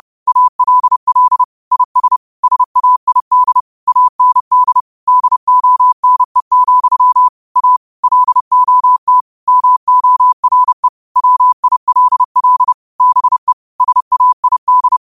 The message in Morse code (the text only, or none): ALL IS SAID AND DONE, A LOT MORE WILL BE SAID